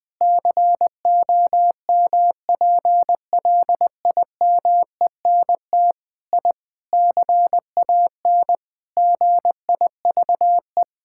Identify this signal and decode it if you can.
Morse code, heard as a keyed tone, about 20 wpm, 700 Hz. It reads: COMPLIMENT I CAN GIVE
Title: Sentences with 8-Word Phrases - 20wpm